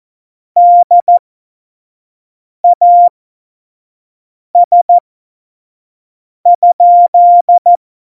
Morse code, heard as a keyed tone, about 14 wpm, spaced as for 5 wpm, 700 Hz. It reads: DAS?